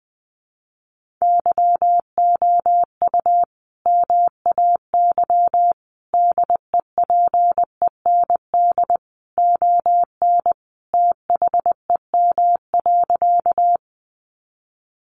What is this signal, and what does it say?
Morse code, heard as a keyed tone, about 20 wpm, 700 Hz. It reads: YOU MAY DEPEND ON THEM.
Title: Sentences from Top 1300 Words 20wpm